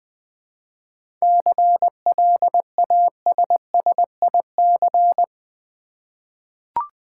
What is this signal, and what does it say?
Morse code, heard as a keyed tone, about 21 wpm, 700 Hz. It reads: CLASSIC
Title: Seven Letter Words - 20wpm